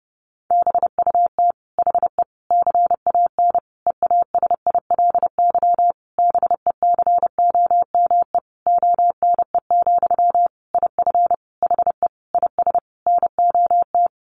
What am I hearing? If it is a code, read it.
Morse code, heard as a keyed tone, about 30 wpm, 700 Hz. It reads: BUT HE CAN EASILY BECOME ONE, IF HE IS NOT
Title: Sentences from Top 900 Words 30wpm